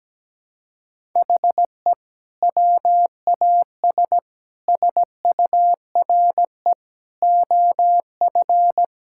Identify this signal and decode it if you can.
Morse code, heard as a keyed tone, about 17 wpm, 700 Hz. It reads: HE WAS SURE OF
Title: Sentences from Top 1000 Words - Review Part 6 - 17wpm